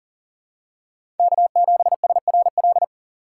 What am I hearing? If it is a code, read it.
Morse code, heard as a keyed tone, about 40 wpm, 700 Hz. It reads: K7SRL
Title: USA Call Signs - 5 Characters - 3x Character Spacing - 40wpm